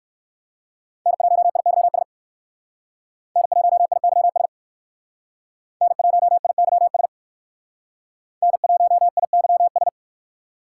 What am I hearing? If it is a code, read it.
Morse code, heard as a keyed tone, about 59 wpm, 700 Hz. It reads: N1IYS N1IYS N1IYS N1IYS